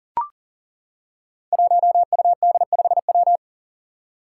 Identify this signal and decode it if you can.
Morse code, heard as a keyed tone, about 38 wpm, 700 Hz. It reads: 1UD5W
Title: Five Challenging Letter-Number Combinations 40wpm